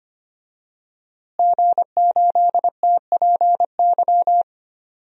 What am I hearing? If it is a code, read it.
Morse code, heard as a keyed tone, about 25 wpm, 700 Hz. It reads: G8TPY
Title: DXCC-100 Call Signs - 5 Characters - Repeated 3x - 25wpm